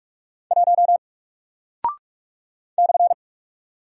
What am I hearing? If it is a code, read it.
Morse code, heard as a keyed tone, about 44 wpm, 700 Hz. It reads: 1 /
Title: Single Character (Letter:Number:Punctuation:Prosign) 45wpm